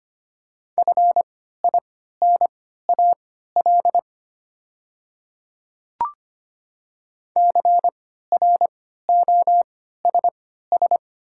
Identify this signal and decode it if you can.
Morse code, heard as a keyed tone, about 25 wpm, spaced as for 14 wpm, 700 Hz. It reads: FINAL CROSS
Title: 3x Character Spacing - Top 500 Words - 25wpm